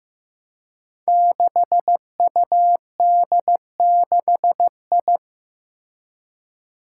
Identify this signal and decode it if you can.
Morse code, heard as a keyed tone, about 15 wpm, 700 Hz. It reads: 6UD6I
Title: Five Challenging Letter-Number Combinations 15wpm